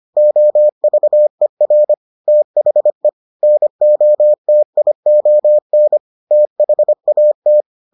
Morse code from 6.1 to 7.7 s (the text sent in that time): THAT